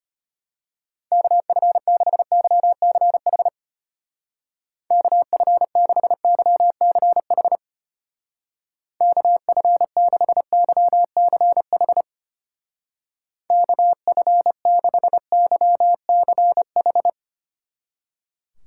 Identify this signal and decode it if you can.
Morse code, heard as a keyed tone, about 32 wpm, 700 Hz. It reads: KF6YCH KF6YCH KF6YCH KF6YCH